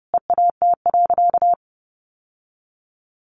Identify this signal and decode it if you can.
Morse code, heard as a keyed tone, about 30 wpm, 700 Hz. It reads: EAT.
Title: Sentences from Top 1500 Words 30wpm